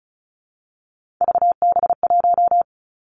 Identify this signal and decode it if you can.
Morse code, heard as a keyed tone, about 36 wpm, 700 Hz. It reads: VB1
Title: Three Challenging Letter-Number Combinations 35wpm